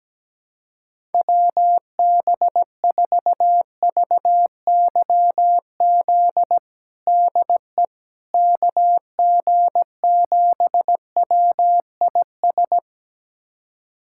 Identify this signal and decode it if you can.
Morse code, heard as a keyed tone, about 17 wpm, 700 Hz. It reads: WB4VYZ DE KG7WIS
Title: USA Callsign DE Callsign 17wpm